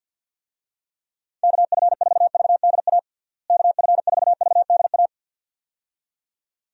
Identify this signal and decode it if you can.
Morse code, heard as a keyed tone, about 51 wpm, 700 Hz. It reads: KF4VDA KF4VDA
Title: USA Call Signs - 6 Characters - Repeated 2x - 50wpm